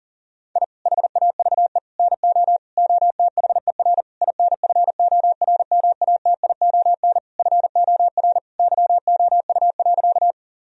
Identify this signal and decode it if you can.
Morse code, heard as a keyed tone, about 40 wpm, 700 Hz. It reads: I HAVE NO OTHER INFORMATION FOR YOU.
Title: Sentences from Top 1000 Words - Review Part 3 - 40wpm